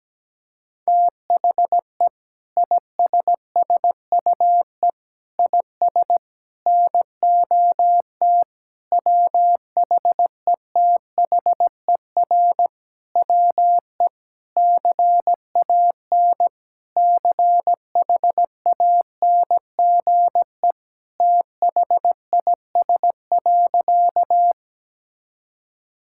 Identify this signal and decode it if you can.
Morse code, heard as a keyed tone, about 17 wpm, 700 Hz. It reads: THE ISSUE IS NOT WHETHER WE CAN CHANGE THIS.
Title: Sentences from Top 1000 Words - Review Part 6 - 17wpm